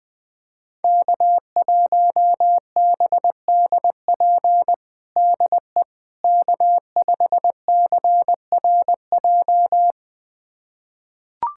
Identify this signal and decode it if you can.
Morse code, heard as a keyed tone, about 20 wpm, 700 Hz. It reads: K1BDP DE K5CRJ
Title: USA Callsign DE Callsign - 5 Characters - 20wpm